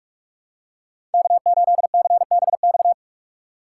Morse code, heard as a keyed tone, about 45 wpm, 700 Hz. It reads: K8CBX
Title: USA Call Signs - 4x Character Spacing - 45wpm